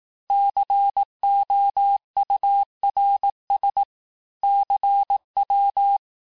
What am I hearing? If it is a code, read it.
Morse code, heard as a keyed tone, about 18 wpm, 800 Hz. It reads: COURS CW
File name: cours-du-lundi files / cours-1588-texte-chiffre-vitesse-17MPM